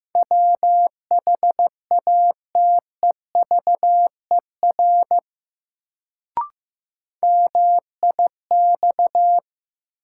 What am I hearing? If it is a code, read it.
Morse code, heard as a keyed tone, about 15 wpm, 700 Hz. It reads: WHATEVER MIX
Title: Sentences from Top 1100 Words - New Word Drill Alt - 15wpm